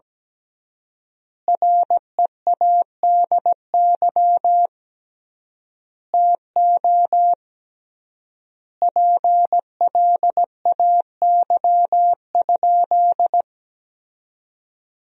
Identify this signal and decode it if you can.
Morse code, heard as a keyed tone, about 17 wpm, 700 Hz. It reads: READY TO PLAY?
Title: 3x Word-Spacing - Sentences from Top 400 Words - 17wpm